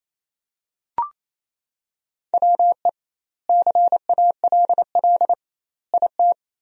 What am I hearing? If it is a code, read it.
Morse code, heard as a keyed tone, about 28 wpm, 700 Hz. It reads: WE CALL IT
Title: 2x Word-Spacing - Sentences from Top 400 Words - 28wpm